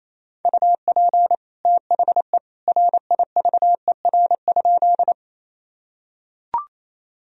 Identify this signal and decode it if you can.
Morse code, heard as a keyed tone, about 28 wpm, 700 Hz. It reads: UP THE RIVER?
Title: Sentences from Top 1000 Words - Review Part 1 - 28wpm